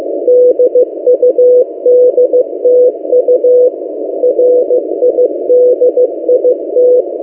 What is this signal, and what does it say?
Morse code, heard as a keyed tone, about 15 wpm, 500 Hz. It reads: DUDTU RIDIT
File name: M08a- -1355utc-8096khz